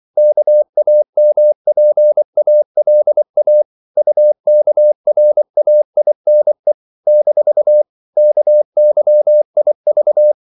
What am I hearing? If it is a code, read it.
Morse code, heard as a keyed tone, about 24 wpm, 600 Hz. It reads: KAMPALA UKRAINE - KYIV
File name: CW / Practice / NationalCapitals24